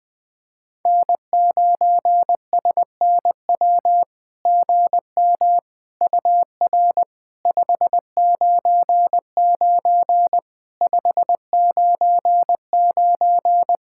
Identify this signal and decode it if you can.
Morse code, heard as a keyed tone, about 21 wpm, 700 Hz. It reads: N9SNW GM UR 599 599
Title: SOTA QSO Practice 20wpm